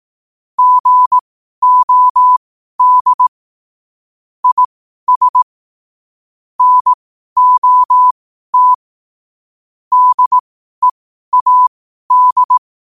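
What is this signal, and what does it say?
Morse code, heard as a keyed tone, about 18 wpm, spaced as for 12 wpm, 1000 Hz. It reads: GOD IS NOT DEAD
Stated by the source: 12 wpm Morse.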